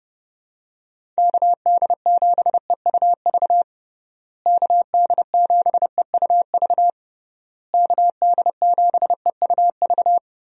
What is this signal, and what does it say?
Morse code, heard as a keyed tone, about 30 wpm, 700 Hz. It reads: KD7EUV KD7EUV KD7EUV
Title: USA Call Signs - 6 Characters - Repeated 3x - 30wpm